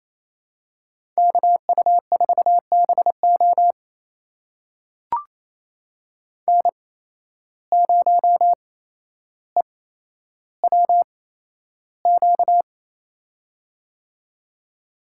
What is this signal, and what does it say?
Morse code, heard as a keyed tone, about 28 wpm, spaced as for 12 wpm, 700 Hz. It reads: KU4BO N0EWQ
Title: USA Call Signs - 5 Characters - 8x Character Spacing - 28wpm